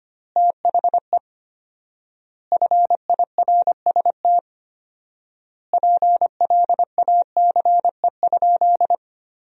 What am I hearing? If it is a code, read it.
Morse code, heard as a keyed tone, about 25 wpm, 700 Hz. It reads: THE FIRST PLACE?
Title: 4x Word-Spacing - Sentences from Top 200 Words - 25wpm